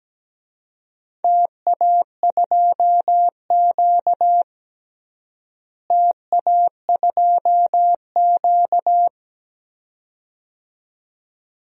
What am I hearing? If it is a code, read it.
Morse code, heard as a keyed tone, about 17 wpm, 700 Hz. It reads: TA2Q TA2Q
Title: DXCC-100 Call Signs - 4 Characters - Repeated 2x - 17wpm